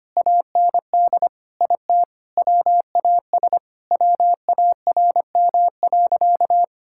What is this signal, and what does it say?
Morse code, heard as a keyed tone, about 25 wpm, 700 Hz. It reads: AND IT WAS WARM.
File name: practice / 25wpm / Sentences with 2-Word Phrases - 25wpm